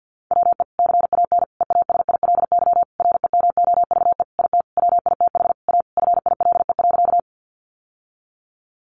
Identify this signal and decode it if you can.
Morse code, heard as a keyed tone, about 50 wpm, 700 Hz. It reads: WE CAN EASILY REMOVE IT WITH A FILE.
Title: Sentences from Top 1200 Words 50wpm